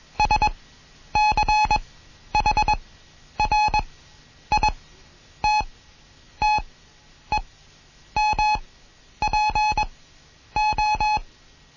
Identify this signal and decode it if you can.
Morse code, heard as a keyed tone, about 21 wpm, spaced as for 10 wpm, 850 Hz. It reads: SCHRITTEMPO
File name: Morsen / Kurs A 20